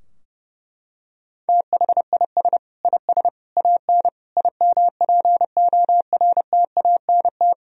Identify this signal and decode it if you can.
Morse code, heard as a keyed tone, about 32 wpm, 700 Hz. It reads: THIS IS AN IMPORTANT
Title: Sentences from Top 1000 Words - Review Part 2 - 30wpm